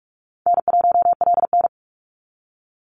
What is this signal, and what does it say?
Morse code, heard as a keyed tone, about 46 wpm, 700 Hz. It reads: N1LN
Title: Easy Call Signs - 45wpm